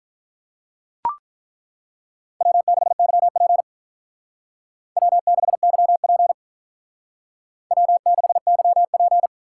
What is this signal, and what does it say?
Morse code, heard as a keyed tone, about 43 wpm, 700 Hz. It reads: W6YP W6YP W6YP
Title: Speed Racing - USA Call Signs 35wpm